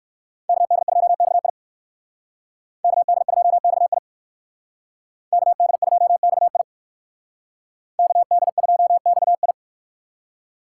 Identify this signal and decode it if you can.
Morse code, heard as a keyed tone, about 56 wpm, 700 Hz. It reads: KD2XI KD2XI KD2XI KD2XI
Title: Speed Racing - USA Call Signs 45wpm